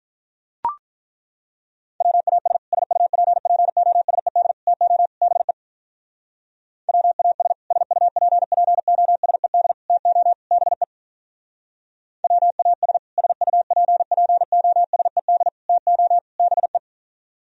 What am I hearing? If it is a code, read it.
Morse code, heard as a keyed tone, about 42 wpm, 700 Hz. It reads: WAS SUPPOSED TO BE WAS SUPPOSED TO BE WAS SUPPOSED TO BE
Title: Speed-Racing - Sets of 4 Words 35wpm